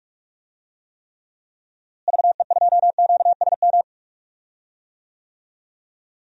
Morse code, heard as a keyed tone, about 45 wpm, 700 Hz. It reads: VE2QSM
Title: DXCC-100 Call Signs - 6 Characters - Repeated 2x - 45wpm